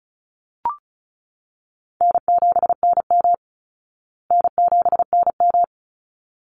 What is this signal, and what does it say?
Morse code, heard as a keyed tone, about 34 wpm, 700 Hz. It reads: N7NM N7NM
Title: Call Signs Repeated 3x - 35wpm